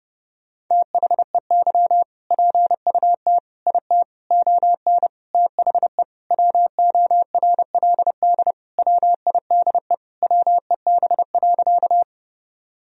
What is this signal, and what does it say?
Morse code, heard as a keyed tone, about 30 wpm, 700 Hz. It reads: THEY PUT IT ON THE WORLD WIDE WEB.
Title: Sentences from Top 1000 Words - Review Part 4 - 30wpm